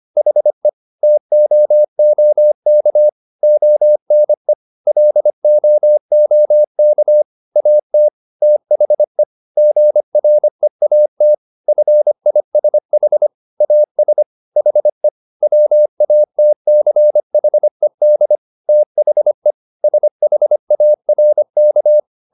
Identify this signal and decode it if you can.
Morse code, heard as a keyed tone, about 25 wpm, 600 Hz. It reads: HE TOOK ONE LOOK AT THE GREAT FISH AS HE WATCHED THE SHARK